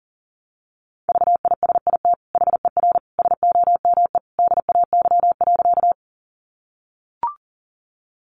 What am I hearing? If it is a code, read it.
Morse code, heard as a keyed tone, about 40 wpm, 700 Hz. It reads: VISIT HER SOME DAY.
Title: Sentences from Top 900 Words 40wpm